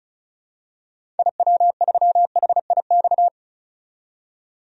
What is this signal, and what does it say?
Morse code, heard as a keyed tone, about 36 wpm, 700 Hz. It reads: IW3HIX
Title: DXCC-100 Call Signs - 6 Characters - 35wpm